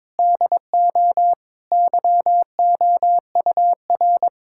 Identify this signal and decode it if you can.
Morse code, heard as a keyed tone, about 21 wpm, 700 Hz. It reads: DO YOUR
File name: practice / 22wpm / Sentences with 2-Word Phrases - 22wpm